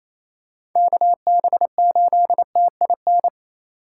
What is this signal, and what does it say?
Morse code, heard as a keyed tone, about 28 wpm, 700 Hz. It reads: KB8TIN
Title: USA Call Signs - 6 Characters - 2x Character Spacing - 28wpm